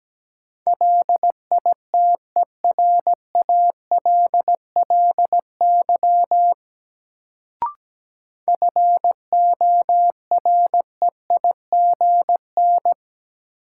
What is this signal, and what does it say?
Morse code, heard as a keyed tone, about 17 wpm, 700 Hz. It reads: LITERALLY FOREIGN
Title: Sentences from Top 1400 Words - New Word Drill Alt - 17wpm